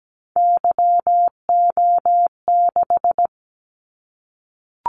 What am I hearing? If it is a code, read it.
Morse code, heard as a keyed tone, about 17 wpm, 700 Hz. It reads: YO6
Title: Callsign Prefixes for Top 25 DXCC - Repeated 3x - 17wpm